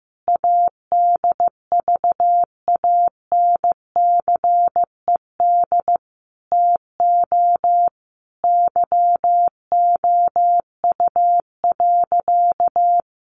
Morse code, heard as a keyed tone, about 15 wpm, 700 Hz. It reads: ADVANCED TO YOU.
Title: Sentences from Top 1200 Words 15wpm